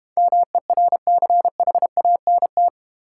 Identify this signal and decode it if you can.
Morse code, heard as a keyed tone, about 30 wpm, 700 Hz. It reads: MERCHANT
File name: Easy Words - Speed-Racing with Hints - 25wpm